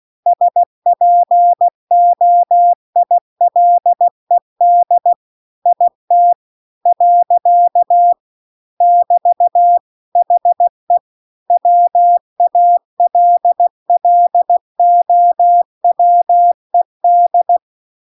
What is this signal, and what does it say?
Morse code, heard as a keyed tone, about 16 wpm, 700 Hz. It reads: SPOILED IT . <BT> HE WALLOWED